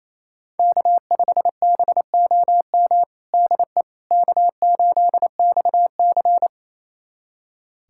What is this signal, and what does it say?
Morse code, heard as a keyed tone, about 28 wpm, 700 Hz. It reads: K5BOM DE K8XC
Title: USA Callsign DE Callsign 28wpm